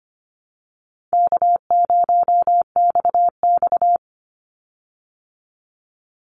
Morse code, heard as a keyed tone, about 25 wpm, 700 Hz. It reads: K0XX
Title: Easy Call Signs Repeated 2x - 25wpm